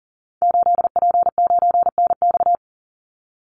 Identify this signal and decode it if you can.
Morse code, heard as a keyed tone, about 40 wpm, 700 Hz. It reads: 8P9NX
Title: Call Signs Repeated 2x - 40wpm